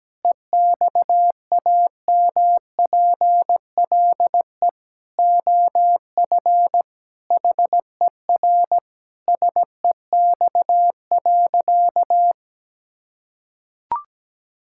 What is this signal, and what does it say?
Morse code, heard as a keyed tone, about 17 wpm, 700 Hz. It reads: EXAMPLE OF HER SEX.
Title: Sentences from Top 1600 Words 17wpm